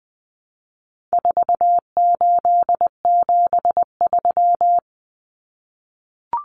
Morse code, heard as a keyed tone, about 20 wpm, 700 Hz. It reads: 4873
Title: Common Cut Numbers - 4-Digit 20wpm